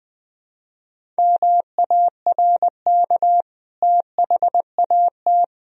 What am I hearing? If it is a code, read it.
Morse code, heard as a keyed tone, about 20 wpm, 700 Hz. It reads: MARK THAT
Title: Sentences from Top 1000 Words - Review Part 1 - 20wpm